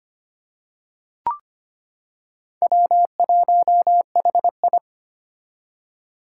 Morse code, heard as a keyed tone, about 25 wpm, 700 Hz. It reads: W1HI
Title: Four Challenging Letter-Number Combinations 25wpm